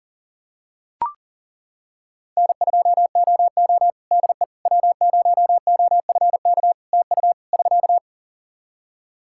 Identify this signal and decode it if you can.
Morse code, heard as a keyed tone, about 40 wpm, 700 Hz. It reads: N2OO DE W0OFK TU <SK>
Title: K1USN SST QSO Practice 40wpm